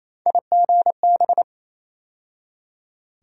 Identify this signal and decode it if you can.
Morse code, heard as a keyed tone, about 28 wpm, 700 Hz. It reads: IGB
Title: USA Call Sign Postfixes - 3 Letters 28wpm